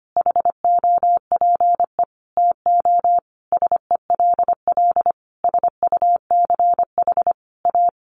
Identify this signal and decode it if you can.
Morse code, heard as a keyed tone, about 25 wpm, 700 Hz. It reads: HOPE TO SELL SUCH A
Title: Sentences from Top 1100 Words 25wpm